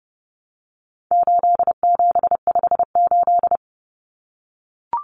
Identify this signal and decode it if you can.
Morse code, heard as a keyed tone, about 30 wpm, 700 Hz. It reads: 8758
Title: Numbers - 4-Digit 30wpm